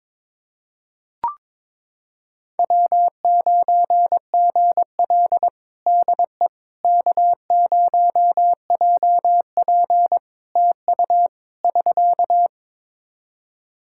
Morse code, heard as a keyed tone, about 22 wpm, 700 Hz. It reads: W9GL DE K0JP TU <SK>
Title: Practice QSO - Exchange 4 - Ending - 22wpm